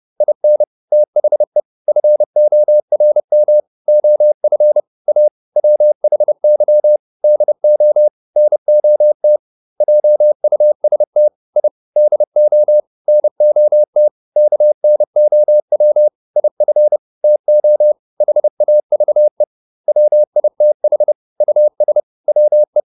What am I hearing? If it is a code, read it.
Morse code, heard as a keyed tone, about 30 wpm, 600 Hz. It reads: IN THE FORM OF A WHY DO NOT JUST I DO NOT KNOW IF TO HAVE WITH US WE